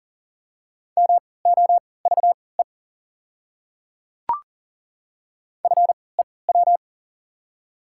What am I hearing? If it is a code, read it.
Morse code, heard as a keyed tone, about 40 wpm, spaced as for 23 wpm, 690 Hz. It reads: MOVE FEW
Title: 3x Character Spacing - Top 300 Words - 40wpm